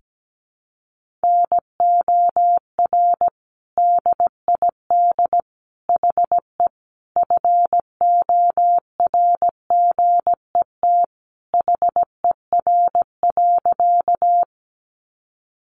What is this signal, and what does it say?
Morse code, heard as a keyed tone, about 17 wpm, 700 Hz. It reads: NOR DID HE FORGET HER.
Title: Sentences from Top 1500 Words 17wpm